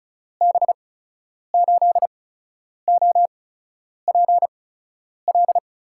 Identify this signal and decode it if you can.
Morse code, heard as a keyed tone, about 36 wpm, spaced as for 10 wpm, 700 Hz. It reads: B8OPL
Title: USA Call Signs - 8x Character Spacing - 35wpm